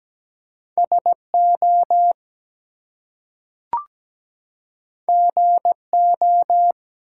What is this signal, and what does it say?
Morse code, heard as a keyed tone, about 17 wpm, 700 Hz. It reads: SO GO
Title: Two Letter Words - 17wpm